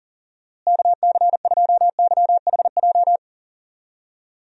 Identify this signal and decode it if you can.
Morse code, heard as a keyed tone, about 38 wpm, 700 Hz. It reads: KC2YHJ
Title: USA Call Signs - 6 Characters - Repeated 3x - 40wpm